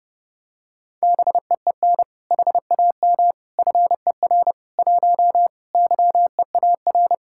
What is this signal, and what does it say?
Morse code, heard as a keyed tone, about 30 wpm, 700 Hz. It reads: BEEN HAM FER 1 YEAR